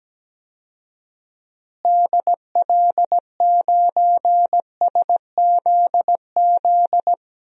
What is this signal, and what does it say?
Morse code, heard as a keyed tone, about 17 wpm, 700 Hz. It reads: DL9SZZ
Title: DXCC-100 Call Signs - 6 Characters - Repeated 2x - 17wpm